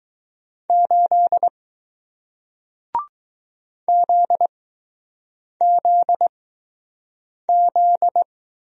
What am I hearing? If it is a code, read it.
Morse code, heard as a keyed tone, about 20 wpm, spaced as for 6 wpm, 700 Hz. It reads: 8 ZZZ